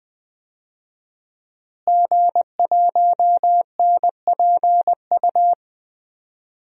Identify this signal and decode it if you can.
Morse code, heard as a keyed tone, about 20 wpm, 700 Hz. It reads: G1NPU
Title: DXCC-100 Call Signs - 5 Characters - Repeated 2x - 20wpm